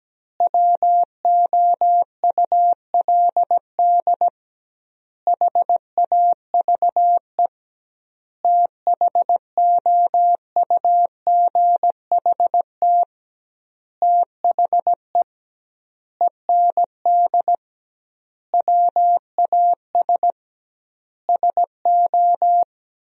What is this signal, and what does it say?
Morse code, heard as a keyed tone, about 17 wpm, 700 Hz. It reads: WOULD HAVE THOUGHT THE END WAS SO
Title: 2x Word-Spacing - Sentences from Top 300 Words - 17wpm